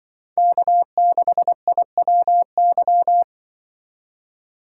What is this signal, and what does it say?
Morse code, heard as a keyed tone, about 24 wpm, 700 Hz. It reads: K6IWY
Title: Speed Racing - USA 1x3 Call Signs 20wpm